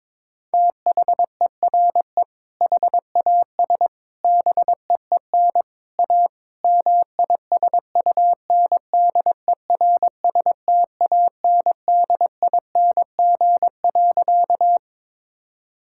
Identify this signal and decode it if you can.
Morse code, heard as a keyed tone, about 22 wpm, 700 Hz. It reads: THERE HAS BEEN A MISUNDERSTANDING.
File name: practice / 22wpm / Sentences with 4-Word Phrases - 22wpm